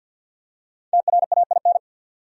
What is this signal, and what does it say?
Morse code, heard as a keyed tone, about 50 wpm, 700 Hz. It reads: TRAIN
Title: Five Letter Words - 50wpm